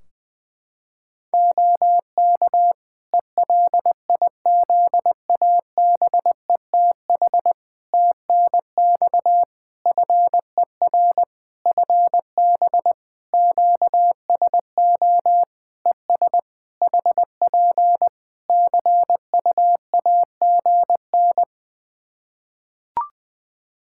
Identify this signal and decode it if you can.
Morse code, heard as a keyed tone, about 20 wpm, 700 Hz. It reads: OK ELIZABETH TNX FER FB QSO ES HP CUAGN
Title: Practice QSO - Exchange 4 - Ending - 20wpm